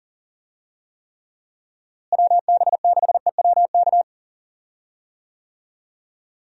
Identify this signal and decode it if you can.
Morse code, heard as a keyed tone, about 40 wpm, 700 Hz. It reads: WB6EWK
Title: USA Call Signs - 6 Characters - Repeated 3x - 40wpm